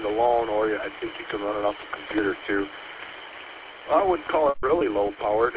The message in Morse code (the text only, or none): none